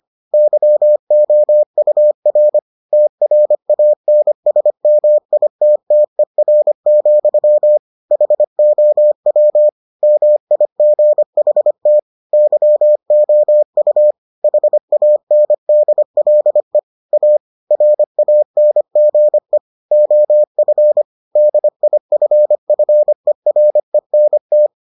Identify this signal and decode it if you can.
Morse code, heard as a keyed tone, about 25 wpm, 600 Hz. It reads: YOUR TRANSMITTER, HOW MIGHT YOU HANDLE A RANGE OF DIFFERENT